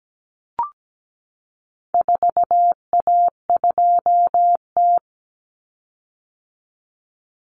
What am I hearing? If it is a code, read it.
Morse code, heard as a keyed tone, about 17 wpm, 700 Hz. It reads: 4A2T
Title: Common Cut Numbers - 4-Digit 17wpm